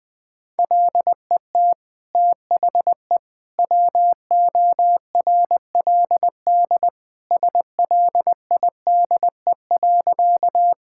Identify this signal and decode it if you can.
Morse code, heard as a keyed tone, about 20 wpm, 700 Hz. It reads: LET THE WORLD SLIDE.
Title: Sentences from Top 1600 Words 20wpm